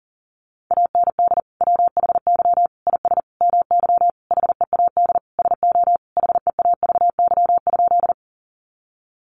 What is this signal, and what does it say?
Morse code, heard as a keyed tone, about 40 wpm, 700 Hz. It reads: AND WHY IS MY HEAD SO HEAVY?
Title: Sentences from Top 1100 Words 40wpm